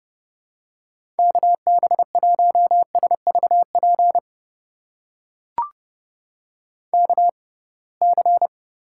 Morse code, heard as a keyed tone, about 30 wpm, spaced as for 23 wpm, 700 Hz. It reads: KB1SVP KC
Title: USA Call Signs - 6 Characters - 6x Character Spacing - 30wpm